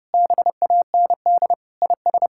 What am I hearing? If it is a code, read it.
Morse code, heard as a keyed tone, about 30 wpm, 700 Hz. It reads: BAND IS
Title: Sentences from Top 1600 Words 30wpm